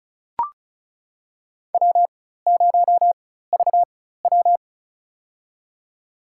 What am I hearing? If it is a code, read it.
Morse code, heard as a keyed tone, about 36 wpm, spaced as for 16 wpm, 700 Hz. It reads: W0VW
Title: USA Call Signs - 4 Characters - 4x Character Spacing - 35wpm